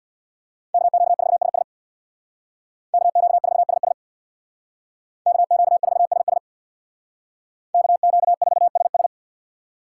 Morse code, heard as a keyed tone, about 60 wpm, 700 Hz. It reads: KQ4SS KQ4SS KQ4SS KQ4SS
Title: Speed Racing - USA 2x2 Call Signs 50wpm